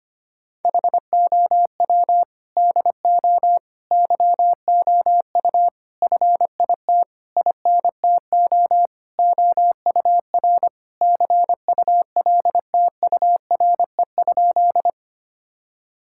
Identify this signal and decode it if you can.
Morse code, heard as a keyed tone, about 25 wpm, 700 Hz. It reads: HOW DO YOU FIT INTO OUR CULTURE?
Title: Sentences from Top 1000 Words - Review Part 6 - 25wpm